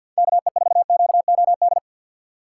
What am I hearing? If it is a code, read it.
Morse code, heard as a keyed tone, about 52 wpm, 700 Hz. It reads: KE4QOD
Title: USA Call Signs - 4x Character Spacing - 50wpm